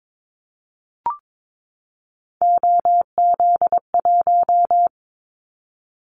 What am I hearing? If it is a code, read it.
Morse code, heard as a keyed tone, about 22 wpm, 700 Hz. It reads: OZ1
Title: Callsign Prefixes for Top 100 DXCC - 22wpm